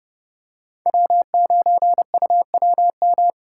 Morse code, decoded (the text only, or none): W9UWM